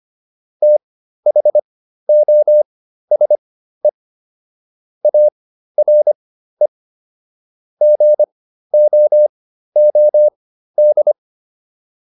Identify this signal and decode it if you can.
Morse code, heard as a keyed tone, about 25 wpm, spaced as for 13 wpm, 600 Hz. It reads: THOSE ARE GOOD